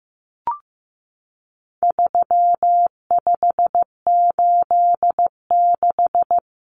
Morse code, heard as a keyed tone, about 15 wpm, 700 Hz. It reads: 3586
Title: Numbers - 4-Digit 15wpm